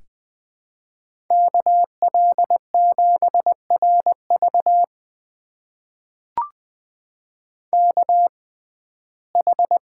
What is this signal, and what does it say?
Morse code, heard as a keyed tone, about 20 wpm, spaced as for 14 wpm, 700 Hz. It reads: KL7RV KH
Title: USA Call Signs - 5 Characters - 6x Character Spacing - 20wpm